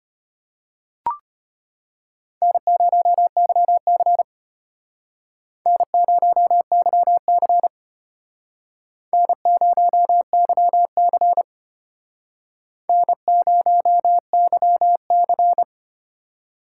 Morse code, heard as a keyed tone, about 33 wpm, 700 Hz. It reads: N0YC N0YC N0YC N0YC